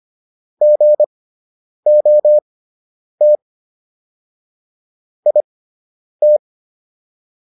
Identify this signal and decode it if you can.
Morse code, heard as a keyed tone, about 25 wpm, spaced as for 9 wpm, 600 Hz. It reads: GOT IT